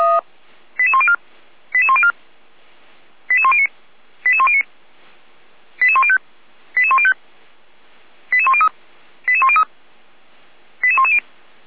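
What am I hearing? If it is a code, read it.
Five-tone selective call (ZVEI), heard as quick runs of five tones, tones of about 70 ms. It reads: 88104 88104 88109 88109 88105 88105 88103 88103 88110
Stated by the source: ZVEI five-tone call 88383 received by SDR.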